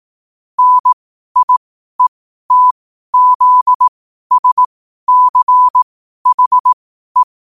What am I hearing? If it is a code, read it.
Morse code, heard as a keyed tone, about 18 wpm, spaced as for 12 wpm, 1000 Hz. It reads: NIETZSCHE